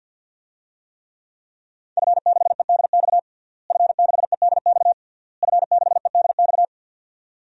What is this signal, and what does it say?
Morse code, heard as a keyed tone, about 46 wpm, 700 Hz. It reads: F6EDX F6EDX F6EDX
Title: DXCC-100 Call Signs - 5 Characters - Repeated 3x - 50wpm